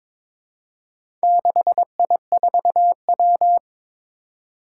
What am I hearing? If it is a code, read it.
Morse code, heard as a keyed tone, about 21 wpm, 700 Hz. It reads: 6I4W
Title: Four Challenging Letter-Number Combinations 22wpm